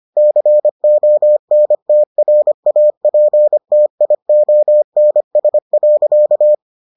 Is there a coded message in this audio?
Yes: Morse code from 0.2 to 6.5 s, heard as a keyed tone, about 25 wpm, 600 Hz. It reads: CONTRAPTIONS.